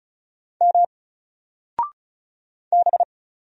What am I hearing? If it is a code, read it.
Morse code, heard as a keyed tone, about 36 wpm, 700 Hz. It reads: M B
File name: Single-Letter-Number-35wpm